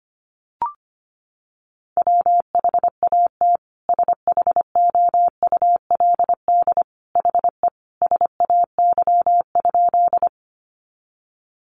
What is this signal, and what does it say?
Morse code, heard as a keyed tone, about 25 wpm, 700 Hz. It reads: WHAT SHOULD HE SAY?
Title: Sentences from Top 300 Words 25wpm